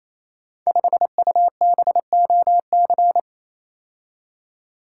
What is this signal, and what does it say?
Morse code, heard as a keyed tone, about 29 wpm, 700 Hz. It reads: 5UBOC
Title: Code Groups - 5 Letter:Numbers - 28wpm